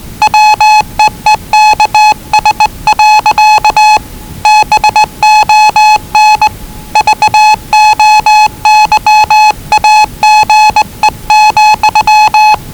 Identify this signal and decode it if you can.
Morse code, heard as a keyed tone, about 20 wpm, 840 Hz. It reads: WEEKS. BON VOYAGE,
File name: WellingtonRadioZlwCloseDownOnHfAndMf / 01-Zlw 4277khz 417.5 qrt until 0925z